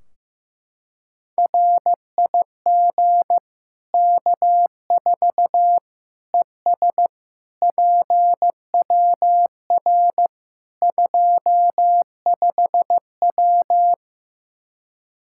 Morse code, heard as a keyed tone, about 15 wpm, 700 Hz. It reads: RIG K4 ES PWR 25W
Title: Practice QSO - Exchange 2 - Info - 15wpm